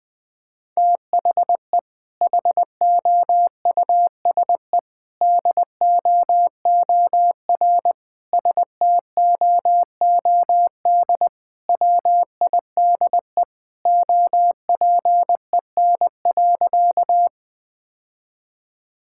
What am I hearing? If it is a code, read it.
Morse code, heard as a keyed tone, about 20 wpm, 700 Hz. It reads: THE HOUSE DOOR STOOD WIDE OPEN.
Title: Sentences from Top 1000 Words - Review Part 6 - 20wpm